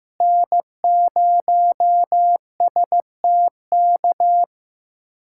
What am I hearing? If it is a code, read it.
Morse code, heard as a keyed tone, about 15 wpm, 700 Hz. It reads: N0STK
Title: USA Callsign DE Callsign 15wpm